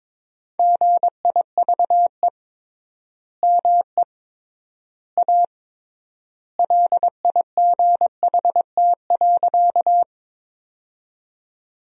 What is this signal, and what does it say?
Morse code, heard as a keyed tone, about 21 wpm, 700 Hz. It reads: GIVE ME A LIGHT.
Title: 3x Word-Spacing - Sentences from Top 200 Words - 22wpm